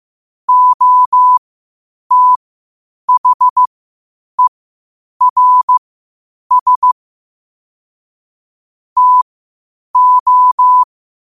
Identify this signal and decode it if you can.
Morse code, heard as a keyed tone, about 15 wpm, spaced as for 8 wpm, 1000 Hz. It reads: OTHERS TO
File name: qotdfiles / 8wpm